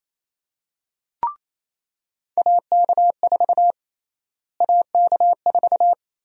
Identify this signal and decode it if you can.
Morse code, heard as a keyed tone, about 28 wpm, 700 Hz. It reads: AK4 AK4
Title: USA Call Sign Prefixes - Repeated 2x - 28wpm